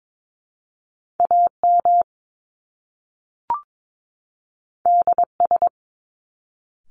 Morse code, heard as a keyed tone, about 22 wpm, 700 Hz. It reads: AM DS
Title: Top Two Letter Combinations in Words 22wpm